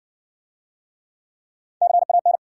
Morse code, heard as a keyed tone, about 58 wpm, 700 Hz. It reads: CAN